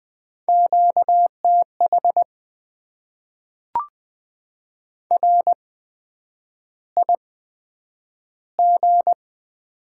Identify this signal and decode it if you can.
Morse code, heard as a keyed tone, about 20 wpm, spaced as for 9 wpm, 700 Hz. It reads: QTH RIG
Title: Top 20 QSO Elements - 8x Character Spacing - 20wpm